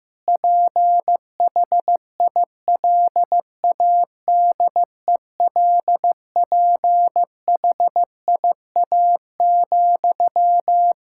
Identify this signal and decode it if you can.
Morse code, heard as a keyed tone, about 15 wpm, 700 Hz. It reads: PHILADELPHIA,
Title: Practice QSO - Exchange 1 - Beginning - 15wpm